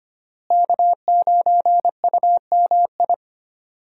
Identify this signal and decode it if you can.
Morse code, heard as a keyed tone, about 25 wpm, 700 Hz. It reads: K9UMI
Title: USA Call Signs - 5 Characters - Repeated 3x - 25wpm